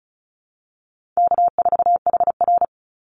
Morse code, heard as a keyed tone, about 34 wpm, 700 Hz. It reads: K4HR
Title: Call Signs Repeated 3x - 35wpm